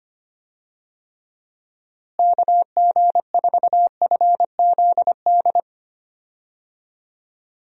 Morse code, heard as a keyed tone, about 25 wpm, 700 Hz. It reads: KG4FZD
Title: USA Call Signs - 6 Characters - Repeated 3x - 25wpm